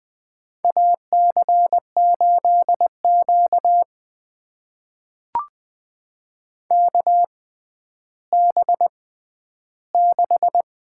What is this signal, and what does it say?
Morse code, heard as a keyed tone, about 20 wpm, spaced as for 11 wpm, 700 Hz. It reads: AC8Q KB6
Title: USA Call Signs - 6x Character Spacing - 20wpm